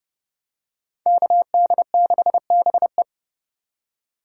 Morse code, heard as a keyed tone, about 27 wpm, 700 Hz. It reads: KD6BE